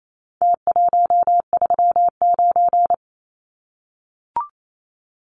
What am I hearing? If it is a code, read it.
Morse code, heard as a keyed tone, about 28 wpm, 700 Hz. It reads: T139
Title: Numbers - Contest Style - 4-Digit 28wpm